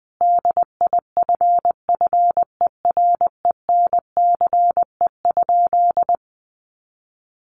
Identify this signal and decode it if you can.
Morse code, heard as a keyed tone, about 20 wpm, 700 Hz. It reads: DIFFERENCE?
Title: Sentences from Top 800 Words 20wpm